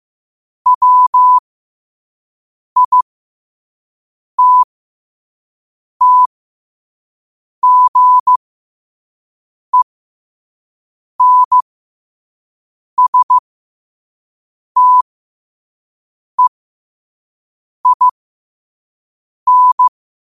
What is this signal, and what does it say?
Morse code, heard as a keyed tone, about 15 wpm, spaced as for 5 wpm, 1000 Hz. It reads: WITTGENSTEIN
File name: qotdfiles / 5wpm